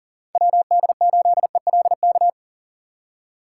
Morse code, heard as a keyed tone, about 49 wpm, 700 Hz. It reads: WD8ELK
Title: Speed Racing - USA 2x3 Call Signs 40wpm